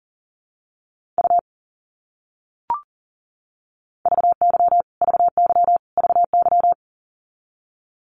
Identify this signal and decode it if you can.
Morse code, heard as a keyed tone, about 37 wpm, 700 Hz. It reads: U VY VY VY
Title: Abbreviations Repeated 3x - 40wpm